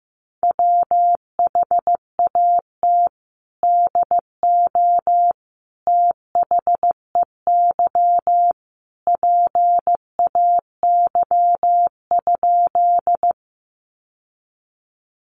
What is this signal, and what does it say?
Morse code, heard as a keyed tone, about 15 wpm, 700 Hz. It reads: WHAT DO THEY PAY?
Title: Sentences from Top 700 Words 15wpm